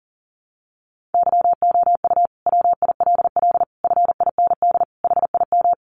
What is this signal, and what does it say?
Morse code, heard as a keyed tone, about 40 wpm, 700 Hz. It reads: YOU WILL FIND HIM
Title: Sentences from Top 1100 Words 40wpm